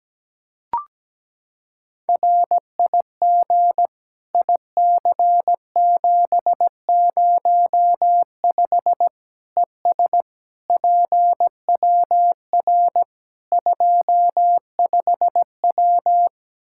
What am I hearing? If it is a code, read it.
Morse code, heard as a keyed tone, about 17 wpm, 700 Hz. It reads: RIG IC705 ES PWR 25W
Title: Practice QSO - Exchange 2 - Info - 17wpm